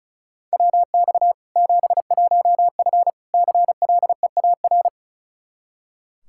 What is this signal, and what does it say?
Morse code, heard as a keyed tone, about 36 wpm, 700 Hz. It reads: WX 71F CLEAR
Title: K1USN SST QSO Practice 35wpm